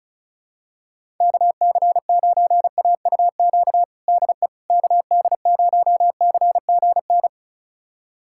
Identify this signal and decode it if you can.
Morse code, heard as a keyed tone, about 34 wpm, 700 Hz. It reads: KC9AUQ DE KD0CGN
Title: USA Callsign DE Callsign - 6 Characters - 35wpm